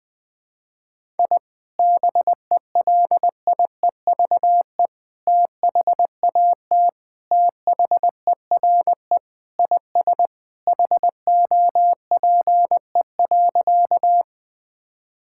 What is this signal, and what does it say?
Morse code, heard as a keyed tone, about 20 wpm, 700 Hz. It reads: I BELIEVE THAT THERE IS HOPE.
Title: Sentences with 3-Word Phrases - 20wpm